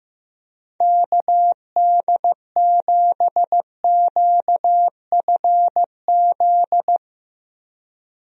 Morse code, heard as a keyed tone, about 15 wpm, 700 Hz. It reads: KD7QFZ
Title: USA Call Signs - 6 Characters - Repeated 3x - 15wpm